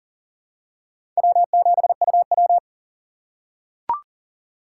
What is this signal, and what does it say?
Morse code, heard as a keyed tone, about 37 wpm, 700 Hz. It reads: W7UW
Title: USA Call Signs - 2x Character Spacing - 40wpm